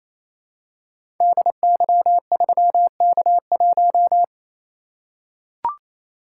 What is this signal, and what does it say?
Morse code, heard as a keyed tone, about 28 wpm, 700 Hz. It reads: DY3K1
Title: Code Groups - 5 Letter:Numbers - 28wpm